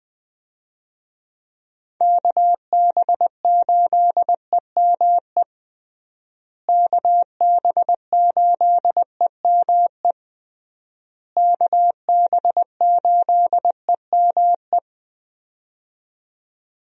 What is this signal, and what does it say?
Morse code, heard as a keyed tone, about 21 wpm, 700 Hz. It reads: KB8EME KB8EME KB8EME
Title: USA Call Signs - 6 Characters - Repeated 3x - 20wpm